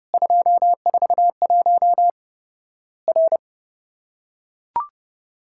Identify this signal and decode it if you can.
Morse code, heard as a keyed tone, about 30 wpm, 700 Hz. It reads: 241 R
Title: Worked All Europe DX Contest - QTC Exchange - 30wpm